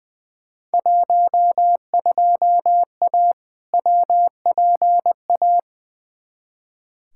Morse code, heard as a keyed tone, about 20 wpm, 700 Hz. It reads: 12A WPA
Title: Field Day Exchanges Encore - 20wpm